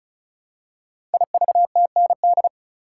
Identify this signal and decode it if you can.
Morse code, heard as a keyed tone, about 36 wpm, 700 Hz. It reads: IVTND
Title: Code Groups - 5 Letters - 35wpm